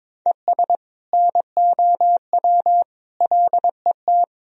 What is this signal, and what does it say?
Morse code, heard as a keyed tone, about 22 wpm, 700 Hz. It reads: ES NOW LET
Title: Sentences from Top 300 Words and Abbreviations 22wpm